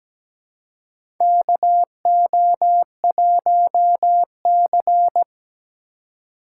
Morse code, heard as a keyed tone, about 17 wpm, 700 Hz. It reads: KO1C
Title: USA Call Signs - 4x Character Spacing - 17wpm